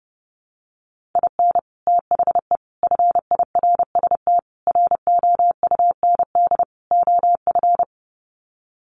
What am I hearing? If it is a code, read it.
Morse code, heard as a keyed tone, about 30 wpm, 700 Hz. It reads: IN THE FIRST ROUND OF
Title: Sets of 5 Words Encore 30wpm